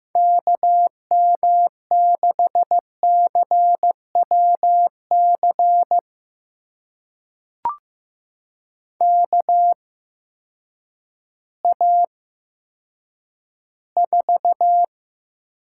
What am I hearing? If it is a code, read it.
Morse code, heard as a keyed tone, about 15 wpm, spaced as for 9 wpm, 700 Hz. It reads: KM6CWC KA4
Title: USA Call Signs - 6 Characters - 8x Character Spacing - 15wpm